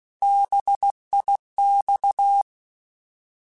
Morse code, heard as a keyed tone, about 16 wpm, 800 Hz. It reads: BIX
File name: Alphabetical / Alphabetical mp3s / Bs @15